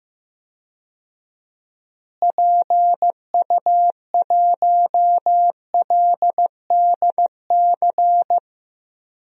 Morse code, heard as a keyed tone, about 15 wpm, 700 Hz. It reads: PU1LDC
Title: DXCC-100 Call Signs - 6 Characters - Repeated 2x - 15wpm